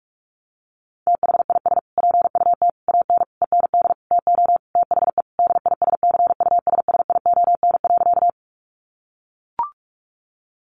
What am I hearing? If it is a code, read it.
Morse code, heard as a keyed tone, about 44 wpm, 700 Hz. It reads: THIS PUT AN END TO THE DISCUSSION.
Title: Sentences from Top 1100 Words 45wpm